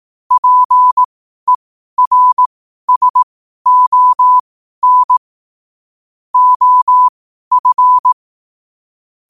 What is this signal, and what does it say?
Morse code, heard as a keyed tone, about 18 wpm, spaced as for 12 wpm, 1000 Hz. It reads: PERSON OF
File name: qotdfiles / 12wpm